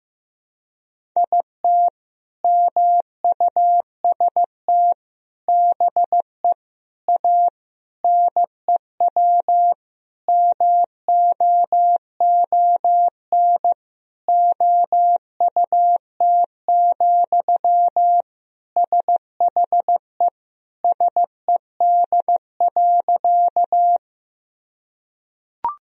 Morse code, heard as a keyed tone, about 15 wpm, 700 Hz. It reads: IT MUST BE A NEW MOON OUT, SHE SED.
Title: Sentences from Top 500 Words and Abbreviations 15wpm